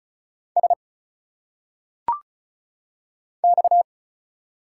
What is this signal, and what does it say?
Morse code, heard as a keyed tone, about 35 wpm, 700 Hz. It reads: S X
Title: Single Letters - Phonetic Pronunciation 35wpm